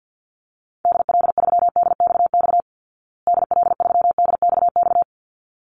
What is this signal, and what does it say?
Morse code, heard as a keyed tone, about 50 wpm, 700 Hz. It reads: DL3DXX DL3DXX
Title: Call Signs Repeated 3x - 50wpm